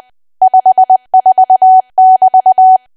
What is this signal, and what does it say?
Morse code, heard as a keyed tone, about 20 wpm, 750 Hz. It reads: 54<BT>
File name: cwa / W1AW-Level3 / 20WPM